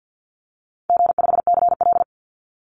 Morse code, heard as a keyed tone, about 50 wpm, 700 Hz. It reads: G5CL
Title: Easy Call Signs Repeated 3x - 50wpm